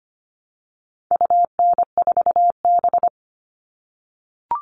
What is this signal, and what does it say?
Morse code, heard as a keyed tone, about 25 wpm, 700 Hz. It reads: UN4B
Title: All Cut Numbers - 4-Digit 25wpm